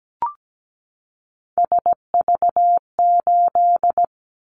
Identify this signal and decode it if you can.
Morse code, heard as a keyed tone, about 17 wpm, 700 Hz. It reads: SV8
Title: Callsign Prefixes for Top 50 DXCC - 17wpm